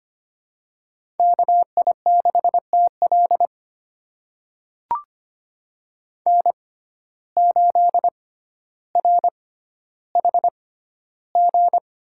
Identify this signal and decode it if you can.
Morse code, heard as a keyed tone, about 25 wpm, spaced as for 13 wpm, 700 Hz. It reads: KI6TL N8RHG
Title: USA Call Signs - 5 Characters - 6x Character Spacing - 25wpm